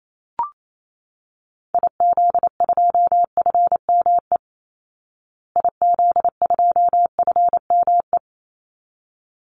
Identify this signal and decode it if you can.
Morse code, heard as a keyed tone, about 28 wpm, 700 Hz. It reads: IZ2FME IZ2FME
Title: Call Signs Repeated 3x - 28wpm